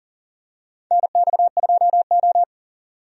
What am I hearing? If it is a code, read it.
Morse code, heard as a keyed tone, about 40 wpm, 700 Hz. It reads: NX2O
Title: USA Call Signs - 4 Characters - 40wpm